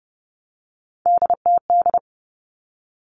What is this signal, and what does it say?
Morse code, heard as a keyed tone, about 30 wpm, 700 Hz. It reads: DTD
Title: Three Challenging Letter-Number Combinations 30wpm